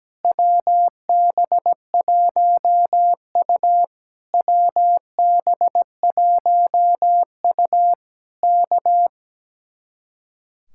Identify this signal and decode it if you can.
Morse code, heard as a keyed tone, about 17 wpm, 700 Hz. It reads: WB1U WB1U K
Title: Speed-Racing Common QSO Phrases - 17wpm